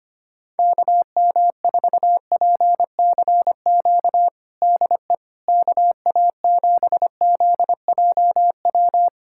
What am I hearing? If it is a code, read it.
Morse code, heard as a keyed tone, about 25 wpm, 700 Hz. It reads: KM4PCQ DE KA7ZJW
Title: USA Callsign DE Callsign - 6 Characters - 25wpm